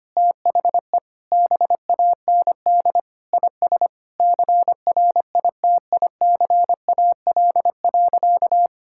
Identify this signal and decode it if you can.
Morse code, heard as a keyed tone, about 25 wpm, 700 Hz. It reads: THE BAND IS CRITICAL.
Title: Sentences from Top 1600 Words 25wpm